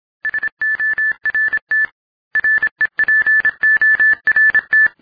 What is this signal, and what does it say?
Morse code, heard as a keyed tone, about 26 wpm, 1750 Hz. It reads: SORT REPORT